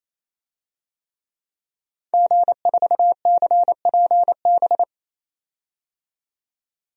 Morse code, heard as a keyed tone, about 28 wpm, 700 Hz. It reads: G4CPB
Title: DXCC-100 Call Signs - 5 Characters - 28wpm